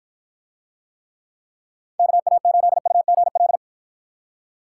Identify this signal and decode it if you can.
Morse code, heard as a keyed tone, about 53 wpm, 700 Hz. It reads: KA8UGL